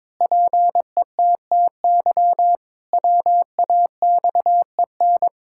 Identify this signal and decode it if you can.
Morse code, heard as a keyed tone, about 21 wpm, 700 Hz. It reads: PETTY WAXEN
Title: Pangram Challenge v2 - 22wpm